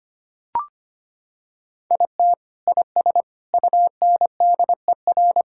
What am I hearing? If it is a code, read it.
Morse code, heard as a keyed tone, about 25 wpm, 700 Hz. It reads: IT IS UNDER
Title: Sentences from Top 1000 Words - Review Part 6 - 25wpm